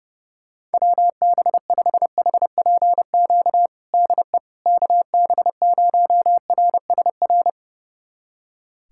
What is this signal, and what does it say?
Morse code, heard as a keyed tone, about 30 wpm, 700 Hz. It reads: WB5HPQ DE KB0RSR
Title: USA Callsign DE Callsign - 6 Characters - 30wpm